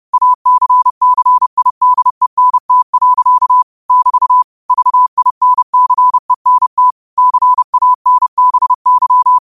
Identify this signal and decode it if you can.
Morse code, heard as a keyed tone, about 30 wpm, 1000 Hz. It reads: ACCIDENT. <BT> VINCENT CANBY
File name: qotdfiles / 30wpm